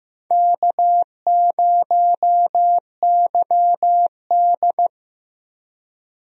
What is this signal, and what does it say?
Morse code, heard as a keyed tone, about 15 wpm, 700 Hz. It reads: K0YD
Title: USA Call Signs - 4 Characters - Repeated 3x - 15wpm